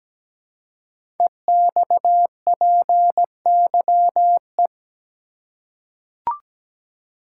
Morse code, heard as a keyed tone, about 17 wpm, 700 Hz. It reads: EXPYE
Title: Code Groups - 5 Letters - 17wpm